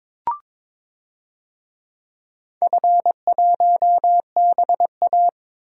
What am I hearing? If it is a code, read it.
Morse code, heard as a keyed tone, about 22 wpm, 700 Hz. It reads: F1BA
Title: DXCC-100 Call Signs - 4 Characters - Repeated 3x - 22wpm